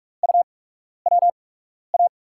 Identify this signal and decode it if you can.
Morse code, heard as a keyed tone, about 44 wpm, spaced as for 12 wpm, 700 Hz. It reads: UWA
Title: USA Call Signs - 5 Characters - 8x Character Spacing - 45wpm